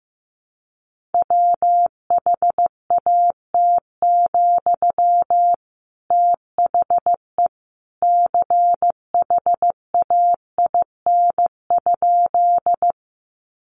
Morse code, heard as a keyed tone, about 15 wpm, 700 Hz. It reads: WHAT, THE CHAIN?
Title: Sentences from Top 1500 Words 15wpm